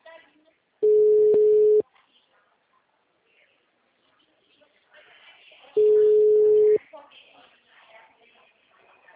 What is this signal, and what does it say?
Ringback tone (Europe, 425 Hz), with speech.